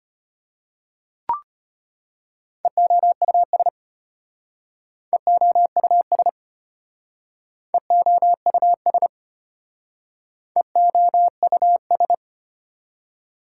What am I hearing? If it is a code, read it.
Morse code, heard as a keyed tone, about 32 wpm, 700 Hz. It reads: EOUS EOUS EOUS EOUS